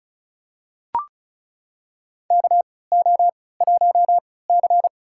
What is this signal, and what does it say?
Morse code, heard as a keyed tone, about 36 wpm, spaced as for 20 wpm, 700 Hz. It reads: KO1C
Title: USA Call Signs - 4 Characters - 3x Character Spacing - 35wpm